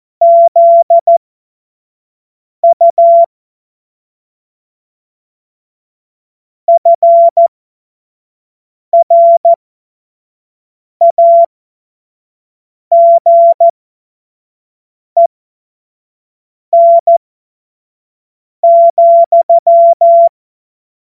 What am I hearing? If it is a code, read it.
Morse code, heard as a keyed tone, about 14 wpm, spaced as for 5 wpm, 680 Hz. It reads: ZU FRAGEN,
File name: Morsekurs / Witze / Witz 01. Juni 2021, 25 F70